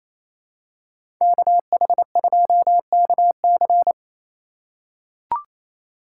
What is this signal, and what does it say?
Morse code, heard as a keyed tone, about 28 wpm, 700 Hz. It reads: KH2KC
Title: USA Call Signs - 5 Characters - 2x Character Spacing - 28wpm